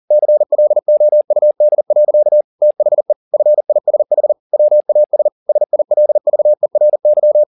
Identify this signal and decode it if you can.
Morse code, heard as a keyed tone, about 40 wpm, 600 Hz. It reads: CLOUD. THE FISH WAS SILVERY
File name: the old man and the sea 2 40wpm